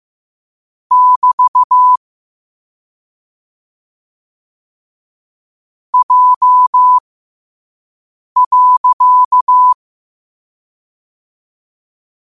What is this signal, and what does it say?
Morse code, heard as a keyed tone, about 15 wpm, spaced as for 5 wpm, 1000 Hz. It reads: <BT> J.
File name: qotdfiles / 5wpm